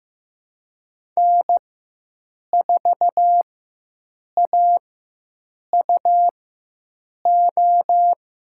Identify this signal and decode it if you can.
Morse code, heard as a keyed tone, about 15 wpm, spaced as for 7 wpm, 700 Hz. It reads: N4AUO
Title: USA Call Signs - 4x Character Spacing - 15wpm